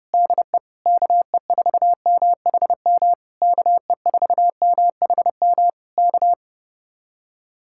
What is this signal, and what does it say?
Morse code, heard as a keyed tone, about 30 wpm, 700 Hz. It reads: DE KE4MHM KE4MHM K
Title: Speed-Racing Common QSO Phrases Encore - 30wpm